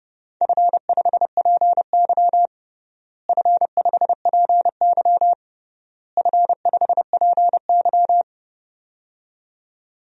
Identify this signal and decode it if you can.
Morse code, heard as a keyed tone, about 30 wpm, 700 Hz. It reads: F5PY F5PY F5PY
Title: DXCC-100 Call Signs - 4 Characters - Repeated 3x - 30wpm